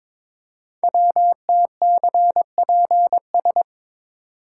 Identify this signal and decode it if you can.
Morse code, heard as a keyed tone, about 22 wpm, 700 Hz. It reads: WTCPS
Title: Code Groups - 5 Letters - 22wpm